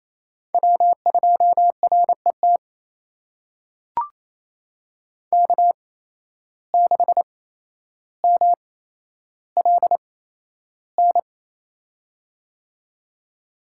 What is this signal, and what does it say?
Morse code, heard as a keyed tone, about 28 wpm, spaced as for 12 wpm, 700 Hz. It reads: W2RET K6MLN
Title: USA Call Signs - 8x Character Spacing - 28wpm